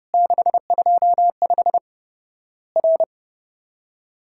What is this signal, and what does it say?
Morse code, heard as a keyed tone, about 30 wpm, 700 Hz. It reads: 625 R
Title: Worked All Europe DX Contest - QTC Exchange - 30wpm